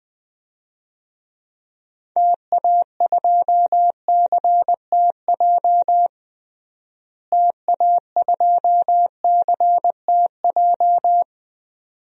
Morse code, heard as a keyed tone, about 20 wpm, 700 Hz. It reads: TA2CTJ TA2CTJ
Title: DXCC-100 Call Signs - 6 Characters - Repeated 2x - 20wpm